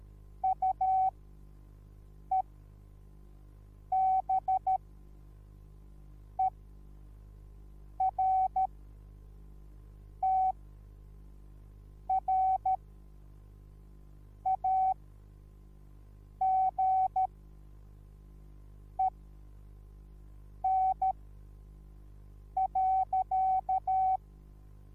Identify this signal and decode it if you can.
Morse code, heard as a keyed tone, about 13 wpm, spaced as for 5 wpm, 760 Hz. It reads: UEBERTRAGEN.
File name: Onlinelehrgang / morsen / DJ4UF Morsen21